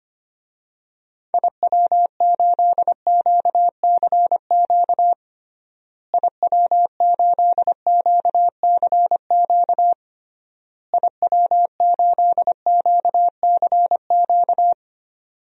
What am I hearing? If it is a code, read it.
Morse code, heard as a keyed tone, about 25 wpm, 700 Hz. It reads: IW8QCQ IW8QCQ IW8QCQ
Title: DXCC-100 Call Signs - 6 Characters - Repeated 3x - 25wpm